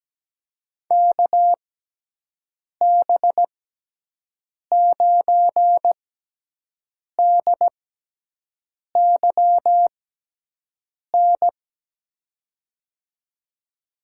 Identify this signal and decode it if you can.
Morse code, heard as a keyed tone, about 17 wpm, spaced as for 6 wpm, 700 Hz. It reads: KB9DYN
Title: USA Call Signs - 6 Characters - 6x Character Spacing - 17wpm